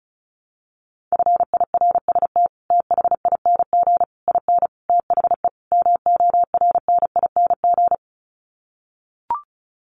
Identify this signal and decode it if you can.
Morse code, heard as a keyed tone, about 34 wpm, 700 Hz. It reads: FIRST THING IN THE MORNING
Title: Sets of 5 Words Encore2 35wpm